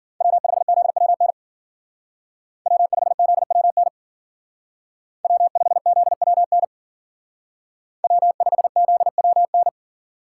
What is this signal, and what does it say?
Morse code, heard as a keyed tone, about 49 wpm, 700 Hz. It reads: W5ZWN W5ZWN W5ZWN W5ZWN
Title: Speed Racing - USA 1x3 Call Signs 40wpm